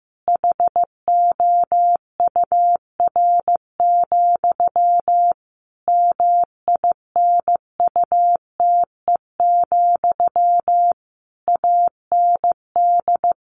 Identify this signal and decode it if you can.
Morse code, heard as a keyed tone, about 15 wpm, 700 Hz. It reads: HOUR, MINUTE, AND
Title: A Sound Of Thunder 15wpm